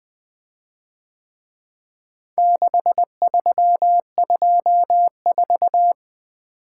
Morse code, heard as a keyed tone, about 23 wpm, 700 Hz. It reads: 6324
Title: Speed-Racing - Four Letter-Number Combinations - 20wpm